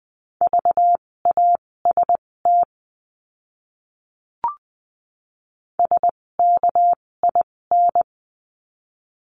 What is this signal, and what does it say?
Morse code, heard as a keyed tone, about 20 wpm, spaced as for 16 wpm, 700 Hz. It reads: VAST SKIN
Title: 256 Lesson 42a - Words 12-20wpm